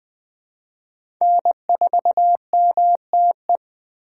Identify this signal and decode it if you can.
Morse code, heard as a keyed tone, about 20 wpm, 700 Hz. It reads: N4MTE
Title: 2023 POTA Activator Call Signs - 20wpm